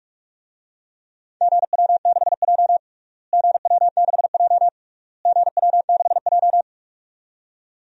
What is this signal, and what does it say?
Morse code, heard as a keyed tone, about 41 wpm, 700 Hz. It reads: GW6J GW6J GW6J
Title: DXCC-100 Call Signs - 4 Characters - Repeated 3x - 45wpm